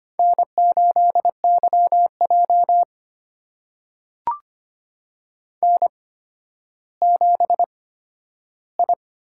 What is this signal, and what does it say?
Morse code, heard as a keyed tone, about 25 wpm, spaced as for 12 wpm, 700 Hz. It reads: N8YJ N7I
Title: USA Call Signs - 4 Characters - 8x Character Spacing - 25wpm